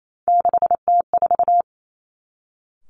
Morse code, heard as a keyed tone, about 28 wpm, 700 Hz. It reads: 6T4
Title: Three Challenging Letter-Number Combinations 28wpm